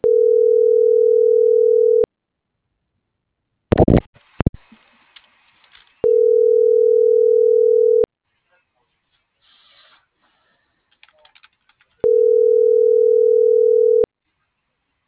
Ringback tone (North America), 440 + 480 Hz.